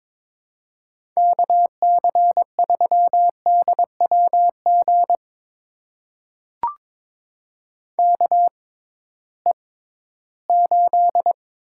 Morse code, heard as a keyed tone, about 21 wpm, spaced as for 14 wpm, 700 Hz. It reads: KC3DWG KE8
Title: USA Call Signs - 6 Characters - 6x Character Spacing - 22wpm